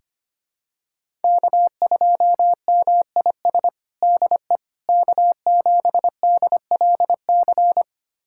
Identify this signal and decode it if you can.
Morse code, heard as a keyed tone, about 25 wpm, 700 Hz. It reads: K2MIS DE K7DLC
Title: USA Callsign DE Callsign - 5 Characters - 25wpm